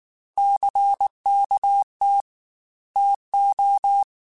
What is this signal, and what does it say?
Morse code, heard as a keyed tone, about 19 wpm, 800 Hz. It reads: CKT TO
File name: Technical Subjects / Technical Subjects-mp3s / Sw Pwr Sup @18